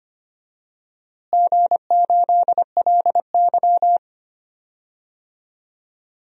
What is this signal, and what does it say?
Morse code, heard as a keyed tone, about 25 wpm, 700 Hz. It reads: G8LY
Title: DXCC-100 Call Signs - 4 Characters - Repeated 3x - 25wpm